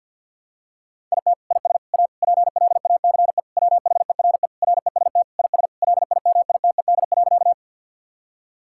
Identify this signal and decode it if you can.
Morse code, heard as a keyed tone, about 50 wpm, 700 Hz. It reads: IT IS A PLACE WHERE REST IS LIMITED.
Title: Sentences from Top 1000 Words - Review Part 6 - 50wpm